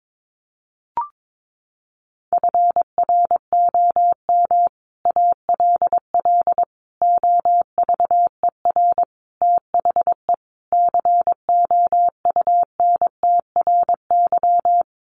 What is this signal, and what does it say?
Morse code, heard as a keyed tone, about 22 wpm, 700 Hz. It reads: FROM ALL OVER THE COUNTRY
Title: Sets of 5 Words Encore 22wpm